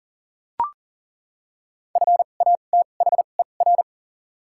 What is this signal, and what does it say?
Morse code, heard as a keyed tone, about 38 wpm, spaced as for 28 wpm, 700 Hz. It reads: FATHER
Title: 2x Character Spacing - Top 200 Words - 40wpm